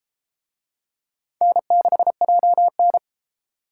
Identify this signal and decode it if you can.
Morse code, heard as a keyed tone, about 29 wpm, 700 Hz. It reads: N6JN